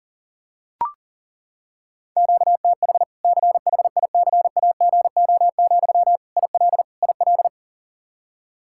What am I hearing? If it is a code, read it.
Morse code, heard as a keyed tone, about 40 wpm, 700 Hz. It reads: QTH CHICAGO, IL IL
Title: Practice QSO - Exchange 1 - Beginning - 40wpm